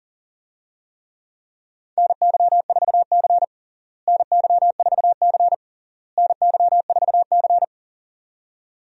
Morse code, heard as a keyed tone, about 40 wpm, 700 Hz. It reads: NY4C NY4C NY4C
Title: USA Call Signs - 4 Characters - Repeated 3x - 40wpm